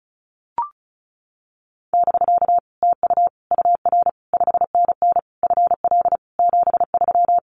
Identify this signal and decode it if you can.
Morse code, heard as a keyed tone, about 34 wpm, 700 Hz. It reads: <BK> TU UR 5NN FL 73
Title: Common QSO Phrases Encore 35wpm